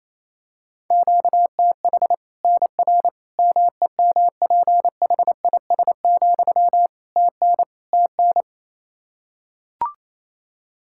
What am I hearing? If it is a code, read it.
Morse code, heard as a keyed tone, about 28 wpm, 700 Hz. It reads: QTH NR MEMPHIS, TN TN
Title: Practice QSO - Exchange 1 - Beginning - 28wpm